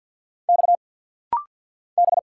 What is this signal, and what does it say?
Morse code, heard as a keyed tone, about 50 wpm, 700 Hz. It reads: X B
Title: Single Letter-Number - Rapid-Fire 50wpm